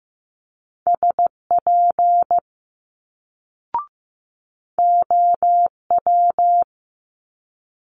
Morse code, heard as a keyed tone, about 15 wpm, 700 Hz. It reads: SP OW
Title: Multi-letter Phonemes Quick Fire 15wpm